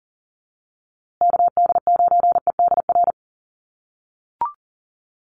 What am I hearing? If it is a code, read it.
Morse code, heard as a keyed tone, about 40 wpm, 700 Hz. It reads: KD9EDR
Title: USA Call Signs Repeated 2x 40wpm